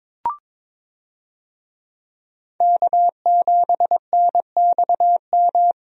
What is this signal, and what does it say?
Morse code, heard as a keyed tone, about 22 wpm, 700 Hz. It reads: K7NXM
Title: USA Call Signs - 5 Characters - Repeated 2x - 22wpm